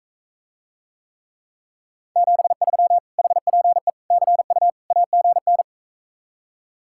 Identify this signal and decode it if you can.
Morse code, heard as a keyed tone, about 45 wpm, 700 Hz. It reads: 73 HPE CU AGN